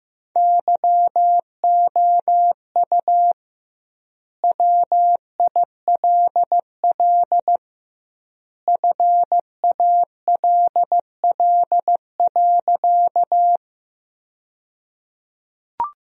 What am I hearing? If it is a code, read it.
Morse code, heard as a keyed tone, about 15 wpm, 700 Hz. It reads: YOU WILL FALL.
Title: 2x Word-Spacing - Sentences from Top 500 Words - 15wpm